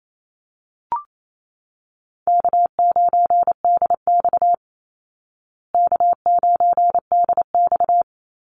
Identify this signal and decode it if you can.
Morse code, heard as a keyed tone, about 28 wpm, 700 Hz. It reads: K9DX K9DX
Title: Call Signs Repeated 3x - 28wpm